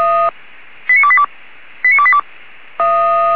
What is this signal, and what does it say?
Five-tone selective call (ZVEI), heard as quick runs of five tones, tones of about 70 ms. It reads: 89282 89282